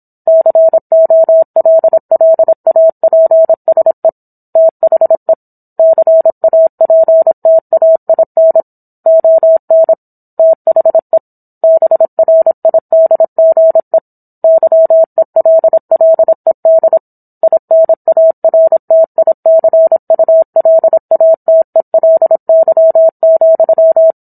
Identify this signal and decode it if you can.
Morse code, heard as a keyed tone, about 27 wpm, 650 Hz. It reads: COLLAPSE THE CAPTAIN ON THE BRIDGE YELLED INARTICULATELY,